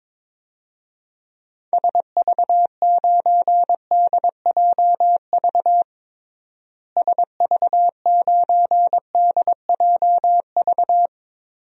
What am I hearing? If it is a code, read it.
Morse code, heard as a keyed tone, about 22 wpm, 700 Hz. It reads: SV9DJV SV9DJV
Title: DXCC-100 Call Signs - 6 Characters - Repeated 2x - 22wpm